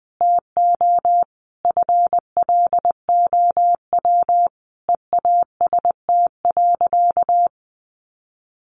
Morse code, heard as a keyed tone, about 20 wpm, 700 Hz. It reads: TO FLOW EAST.
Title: Sentences from Top 1200 Words 20wpm